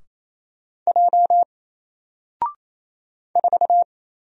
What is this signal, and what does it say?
Morse code, heard as a keyed tone, about 28 wpm, 700 Hz. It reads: J 4
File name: Single-Letter-Number-28wpm